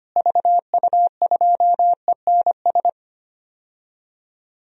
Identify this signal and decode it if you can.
Morse code, heard as a keyed tone, about 25 wpm, 700 Hz. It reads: VU2ENS
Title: DXCC-100 Call Signs - 6 Characters - Repeated 3x - 25wpm